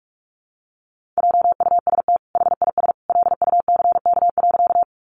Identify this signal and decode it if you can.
Morse code, heard as a keyed tone, about 44 wpm, 700 Hz. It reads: JUST HIS LUCK.
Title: Sentences from Top 1500 Words 45wpm